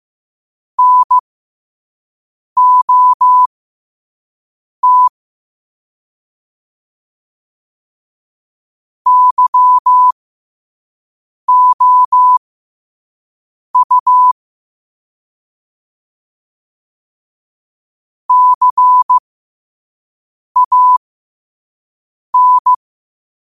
Morse code, heard as a keyed tone, about 15 wpm, spaced as for 5 wpm, 1000 Hz. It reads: NOT YOU CAN
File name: qotdfiles / 5wpm